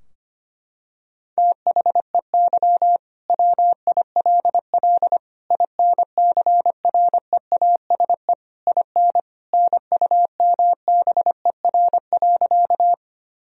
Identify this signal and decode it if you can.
Morse code, heard as a keyed tone, about 25 wpm, 700 Hz. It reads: THEY WILL INCREASE IN NUMBER.
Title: Sentences from Top 1000 Words - Review Part 4 - 25wpm